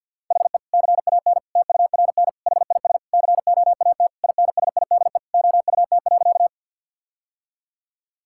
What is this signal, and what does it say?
Morse code, heard as a keyed tone, about 50 wpm, 700 Hz. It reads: HE CAN TURN HIS COAT INSIDE OUT.
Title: Sentences from Top 1600 Words 50wpm